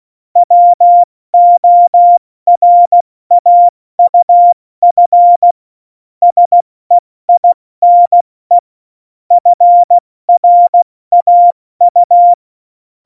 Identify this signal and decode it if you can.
Morse code, heard as a keyed tone, about 16 wpm, 700 Hz. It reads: WORAUF SEINE FRAU
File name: Morsekurs / Witze / Witz 27 Juni 17, Tempo 70 F80